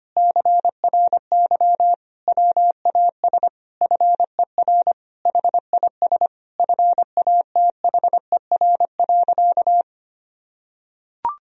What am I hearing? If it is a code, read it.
Morse code, heard as a keyed tone, about 25 wpm, 700 Hz. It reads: CRY WAS FER HIS FATHER.
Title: Sentences from Top 500 Words and Abbreviations 25wpm